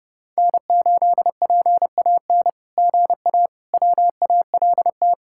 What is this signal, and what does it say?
Morse code, heard as a keyed tone, about 30 wpm, 700 Hz. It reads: N8PAN GA WALT
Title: SOTA QSO Practice 30wpm